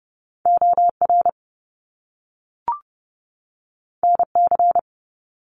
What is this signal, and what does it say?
Morse code, heard as a keyed tone, about 30 wpm, 700 Hz. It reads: OR NC
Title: Top Two Letter Combinations in Words 30wpm